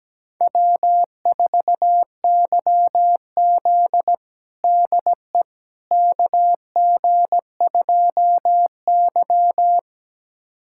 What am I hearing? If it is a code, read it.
Morse code, heard as a keyed tone, about 17 wpm, 700 Hz. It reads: W4YZ DE KG2Y
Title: USA Callsign DE Callsign - 4 Characters - 17wpm